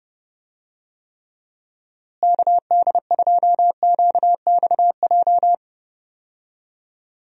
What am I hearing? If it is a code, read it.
Morse code, heard as a keyed tone, about 31 wpm, 700 Hz. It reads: KD2QXJ
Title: USA Call Signs - 6 Characters - Repeated 2x - 30wpm